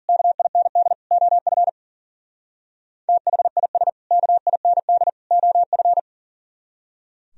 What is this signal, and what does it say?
Morse code, heard as a keyed tone, about 51 wpm, 700 Hz. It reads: KIND OF THIS KIND OF